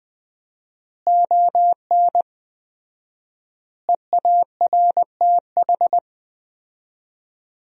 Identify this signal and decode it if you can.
Morse code, heard as a keyed tone, about 20 wpm, 700 Hz. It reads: ON EARTH
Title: 4x Word-Spacing - Sentences from Top 200 Words - 20wpm